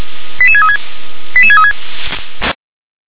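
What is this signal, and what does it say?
Five-tone selective call (ZVEI), heard as quick runs of five tones, tones of about 70 ms. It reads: 88636 88636
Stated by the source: ZVEI five-tone selective call received on SDR.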